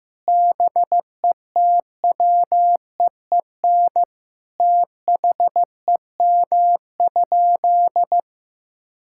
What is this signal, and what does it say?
Morse code, heard as a keyed tone, about 15 wpm, 700 Hz. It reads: BETWEEN THEM?
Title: Sentences from Top 1000 Words - Review Part 6 - 15wpm